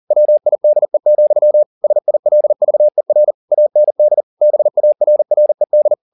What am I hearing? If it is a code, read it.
Morse code, heard as a keyed tone, about 40 wpm, 600 Hz. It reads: WIDE, SILVER AND BARRED
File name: the old man and the sea 2 40wpm